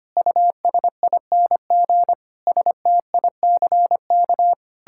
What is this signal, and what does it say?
Morse code, heard as a keyed tone, about 25 wpm, 700 Hz. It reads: USING STICK
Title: Pangram Challenge v2 - 25wpm